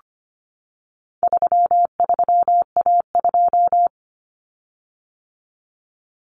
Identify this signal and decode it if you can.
Morse code, heard as a keyed tone, about 25 wpm, 700 Hz. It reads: 33A2
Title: Common Cut Numbers - 4-Digit 25wpm